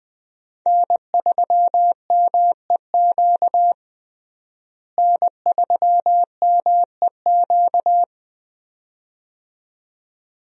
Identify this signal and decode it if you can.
Morse code, heard as a keyed tone, about 20 wpm, 700 Hz. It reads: N3MEQ N3MEQ
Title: USA Call Signs - 5 Characters - Repeated 2x - 20wpm